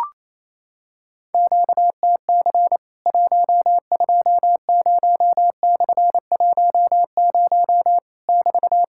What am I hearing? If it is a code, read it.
Morse code, heard as a keyed tone, about 28 wpm, 690 Hz. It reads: QTC 120/10 <BT>
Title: Worked All Europe DX Contest - QTC Exchange - 28wpm